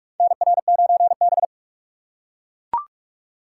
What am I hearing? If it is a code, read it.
Morse code, heard as a keyed tone, about 41 wpm, 700 Hz. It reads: NR9B
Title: USA Call Signs - 3x Character Spacing - 45wpm